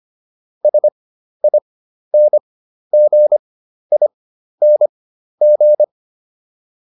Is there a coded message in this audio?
Yes: Morse code from 0.6 to 5.8 s, heard as a keyed tone, about 25 wpm, spaced as for 12 wpm, 600 Hz. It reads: SINGING